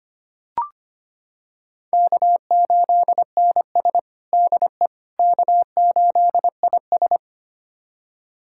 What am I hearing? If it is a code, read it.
Morse code, heard as a keyed tone, about 25 wpm, 700 Hz. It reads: K8NS DE K8IS
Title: USA Callsign DE Callsign - 4 Characters - 25wpm